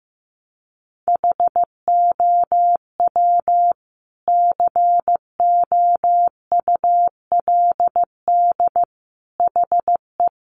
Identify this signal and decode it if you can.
Morse code, heard as a keyed tone, about 15 wpm, 700 Hz. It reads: HOW COULD HE
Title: Sentences from Top 1300 Words 15wpm